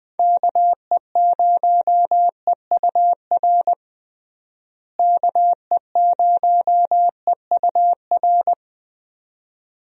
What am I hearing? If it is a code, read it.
Morse code, heard as a keyed tone, about 20 wpm, 700 Hz. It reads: KE0EUR KE0EUR
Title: USA Call Signs - 6 Characters - Repeated 2x - 20wpm